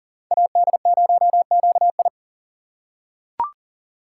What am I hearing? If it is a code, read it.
Morse code, heard as a keyed tone, about 40 wpm, 700 Hz. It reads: AD0QI
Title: USA Call Signs - 2x Character Spacing - 40wpm